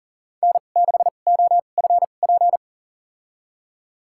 Morse code, heard as a keyed tone, about 40 wpm, spaced as for 29 wpm, 700 Hz. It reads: N6OFP
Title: USA Call Signs - 5 Characters - 2x Character Spacing - 40wpm